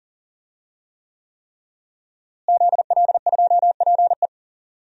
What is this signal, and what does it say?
Morse code, heard as a keyed tone, about 40 wpm, 700 Hz. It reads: ZL2PE
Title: DXCC-100 Call Signs - 5 Characters - Repeated 3x - 40wpm